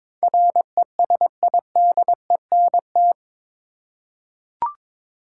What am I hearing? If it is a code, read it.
Morse code, heard as a keyed tone, about 22 wpm, 700 Hz. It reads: RESIDENT
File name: practice / 22wpm / Eight Letter Words - 22wpm